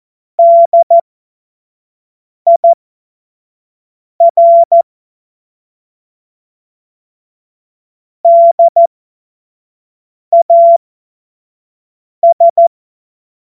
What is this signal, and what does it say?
Morse code, heard as a keyed tone, about 14 wpm, spaced as for 5 wpm, 680 Hz. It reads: DIR DAS
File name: Morsekurs / Witze / Witz 19. Oktober 2021, 25 F70